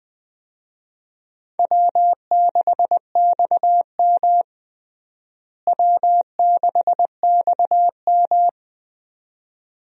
Morse code, heard as a keyed tone, about 20 wpm, 700 Hz. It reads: W6XM W6XM
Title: USA Call Signs - 4 Characters - Repeated 2x - 20wpm